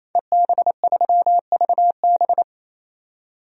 Morse code, heard as a keyed tone, about 28 wpm, 700 Hz. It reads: EB3VB
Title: DXCC-100 Call Signs - 5 Characters - 28wpm